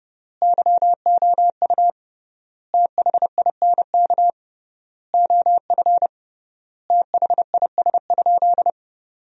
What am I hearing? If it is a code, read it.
Morse code, heard as a keyed tone, about 30 wpm, 700 Hz. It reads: YOU THINK OF THIS?
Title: 3x Word-Spacing - Sentences from Top 200 Words - 30wpm